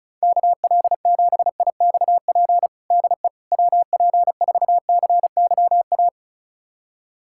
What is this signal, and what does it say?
Morse code, heard as a keyed tone, about 36 wpm, 700 Hz. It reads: KL7IXP DE WP4CYA
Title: USA Callsign DE Callsign - 6 Characters - 35wpm